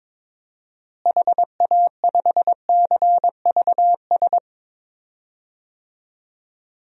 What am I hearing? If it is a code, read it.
Morse code, heard as a keyed tone, about 22 wpm, 700 Hz. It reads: HA5CVS
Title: DXCC-100 Call Signs - 6 Characters - Repeated 2x - 22wpm